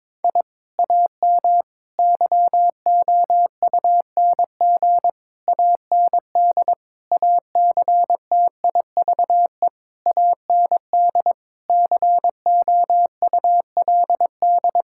Morse code, heard as a keyed tone, about 22 wpm, 700 Hz. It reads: I AM YOUNG AND ACTIVE AND COULD
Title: Sentences from Top 1000 Words - Review Part 6 - 22wpm